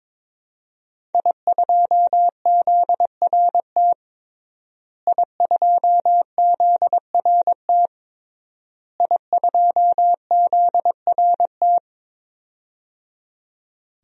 Morse code, heard as a keyed tone, about 22 wpm, 700 Hz. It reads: I2ZRT I2ZRT I2ZRT
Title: DXCC-100 Call Signs - 5 Characters - Repeated 3x - 22wpm